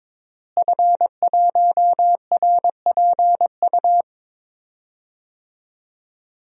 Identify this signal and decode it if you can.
Morse code, heard as a keyed tone, about 22 wpm, 700 Hz. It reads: F1RPU
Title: DXCC-100 Call Signs - 5 Characters - Repeated 2x - 22wpm